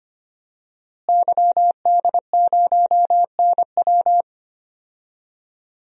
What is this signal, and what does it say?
Morse code, heard as a keyed tone, about 25 wpm, 700 Hz. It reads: YD0NW
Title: DXCC-100 Call Signs - 5 Characters - Repeated 3x - 25wpm